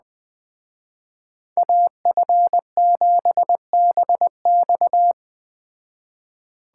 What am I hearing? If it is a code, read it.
Morse code, heard as a keyed tone, about 20 wpm, 700 Hz. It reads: AF7BX
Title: Speed Racing - USA 2x2 Call Signs 17wpm